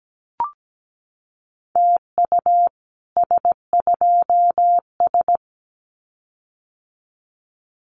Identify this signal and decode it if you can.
Morse code, heard as a keyed tone, about 17 wpm, 700 Hz. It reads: TU S2S
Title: Common QSO Phrases Encore 17wpm